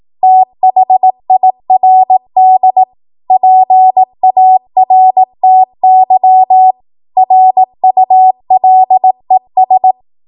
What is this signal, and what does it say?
Morse code, heard as a keyed tone, about 18 wpm, 750 Hz. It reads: THIRD PARTY RULES
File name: Morse / 18WPM